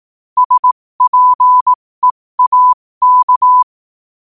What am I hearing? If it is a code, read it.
Morse code, heard as a keyed tone, about 18 wpm, spaced as for 15 wpm, 1000 Hz. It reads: SPEAK
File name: qotdfiles / 15wpm